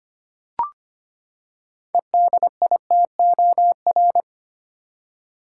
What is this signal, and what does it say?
Morse code, heard as a keyed tone, about 25 wpm, 700 Hz. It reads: EDITOR
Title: Six Letter Words - 25wpm